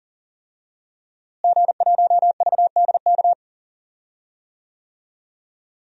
Morse code, heard as a keyed tone, about 40 wpm, 700 Hz. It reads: G1VDK
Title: DXCC-100 Call Signs - 5 Characters - Repeated 3x - 40wpm